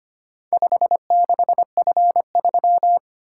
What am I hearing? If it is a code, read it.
Morse code, heard as a keyed tone, about 25 wpm, 700 Hz. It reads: 56F3
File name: practice / 25wpm / Four Letter-Number Combinations - 25wpm